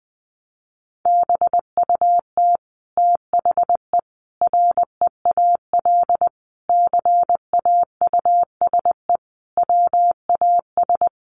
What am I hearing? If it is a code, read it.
Morse code, heard as a keyed tone, about 20 wpm, 700 Hz. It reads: BUT THE REAL CAUSE WAS
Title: Sentences from Top 1200 Words 20wpm